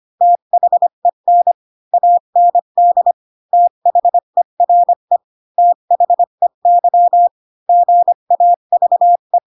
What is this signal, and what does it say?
Morse code, heard as a keyed tone, about 25 wpm, 700 Hz. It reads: THEN AND THERE THEY GAVE